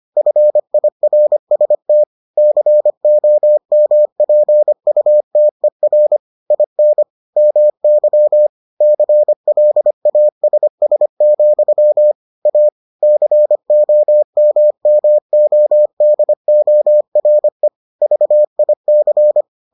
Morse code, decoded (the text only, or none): FIRST COMPUTER IN MY CLASS, A COMMODORE VIC